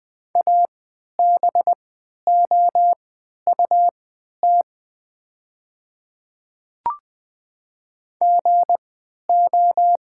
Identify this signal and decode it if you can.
Morse code, heard as a keyed tone, about 20 wpm, spaced as for 11 wpm, 690 Hz. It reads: ABOUT GO
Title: 3x Character Spacing - Top 200 Words - 20wpm